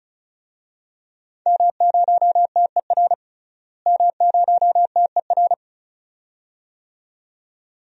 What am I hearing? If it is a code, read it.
Morse code, heard as a keyed tone, about 36 wpm, 700 Hz. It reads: M0TER M0TER
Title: DXCC-100 Call Signs - 5 Characters - Repeated 2x - 35wpm